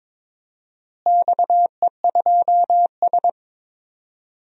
Morse code, heard as a keyed tone, about 22 wpm, 700 Hz. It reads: XE2S
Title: DXCC-100 Call Signs - 4 Characters - Repeated 2x - 22wpm